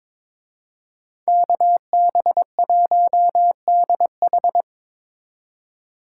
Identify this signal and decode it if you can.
Morse code, heard as a keyed tone, about 25 wpm, 700 Hz. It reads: KB1DH
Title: Speed Racing - USA 2x2 Call Signs 22wpm